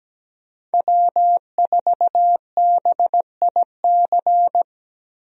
Morse code, heard as a keyed tone, about 17 wpm, 700 Hz. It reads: W4BIC
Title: USA Call Signs - 5 Characters - Repeated 3x - 17wpm